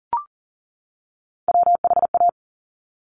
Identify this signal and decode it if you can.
Morse code, heard as a keyed tone, about 39 wpm, 690 Hz. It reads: WHA
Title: Top Three Letter Combinations in Words 40wpm